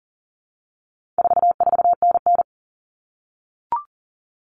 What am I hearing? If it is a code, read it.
Morse code, heard as a keyed tone, about 40 wpm, 700 Hz. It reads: 44NN
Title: Common Cut Numbers - 4-Digit 40wpm